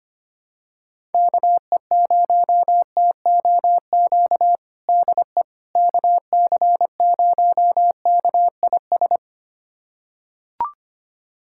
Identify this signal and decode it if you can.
Morse code, heard as a keyed tone, about 25 wpm, 700 Hz. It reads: KE0TOQ DE KC0KIS
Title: USA Callsign DE Callsign - 6 Characters - 25wpm